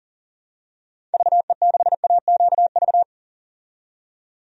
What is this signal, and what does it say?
Morse code, heard as a keyed tone, about 39 wpm, 700 Hz. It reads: VE6AQV
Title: DXCC-100 Call Signs - 6 Characters - 40wpm